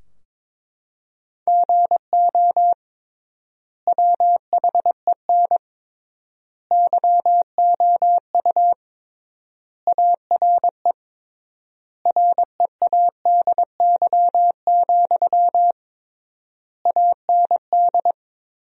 Morse code, heard as a keyed tone, about 22 wpm, 700 Hz. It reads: GO WHEN YOU ARE READY, AND
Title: 3x Word-Spacing - Sentences from Top 500 Words - 22wpm